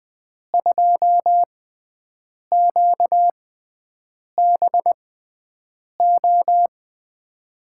Morse code, heard as a keyed tone, about 20 wpm, spaced as for 7 wpm, 700 Hz. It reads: 2QBO
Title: USA Call Signs - 6 Characters - 6x Character Spacing - 20wpm